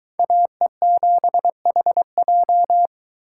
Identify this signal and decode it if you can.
Morse code, heard as a keyed tone, about 23 wpm, 700 Hz. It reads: AE7HJ